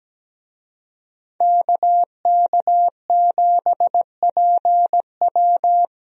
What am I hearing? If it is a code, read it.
Morse code, heard as a keyed tone, about 17 wpm, 700 Hz. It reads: KK7PW
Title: USA Call Signs - 5 Characters - 17wpm